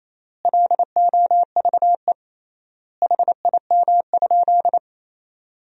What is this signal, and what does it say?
Morse code, heard as a keyed tone, about 28 wpm, 700 Hz. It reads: LOVE HIM?
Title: 3x Word-Spacing - Sentences from Top 400 Words - 28wpm